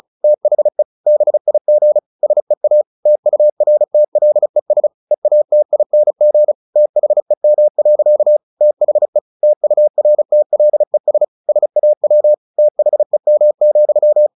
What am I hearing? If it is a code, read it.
Morse code, heard as a keyed tone, about 34 wpm, 600 Hz. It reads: THE BIG SEA TURTLES EATING THEM. THE TURTLES SAW THEM,
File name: the old man and the sea / the old man and the sea 1 35wpm